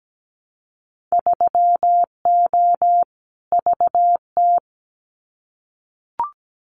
Morse code, heard as a keyed tone, about 17 wpm, 700 Hz. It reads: 3O VT
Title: Winter Field Day Exchanges Encore 17wpm